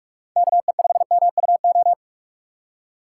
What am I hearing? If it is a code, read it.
Morse code, heard as a keyed tone, about 43 wpm, 700 Hz. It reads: KE5MUO
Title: USA Call Signs - 6 Characters - 8x Character Spacing - 45wpm